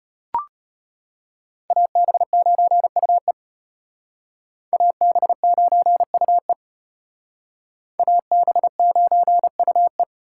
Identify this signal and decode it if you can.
Morse code, heard as a keyed tone, about 32 wpm, 700 Hz. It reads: AB9UE AB9UE AB9UE